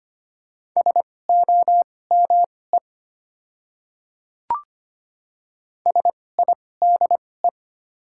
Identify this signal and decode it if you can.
Morse code, heard as a keyed tone, about 25 wpm, spaced as for 18 wpm, 700 Hz. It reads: SOME SIDE
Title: 2x Character Spacing - Top 100 Words - 25wpm